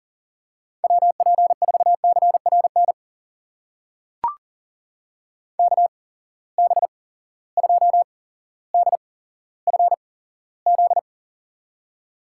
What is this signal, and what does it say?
Morse code, heard as a keyed tone, about 42 wpm, spaced as for 17 wpm, 700 Hz. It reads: WP4CRN KB2DFZ
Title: USA Call Signs - 6 Characters - 8x Character Spacing - 40wpm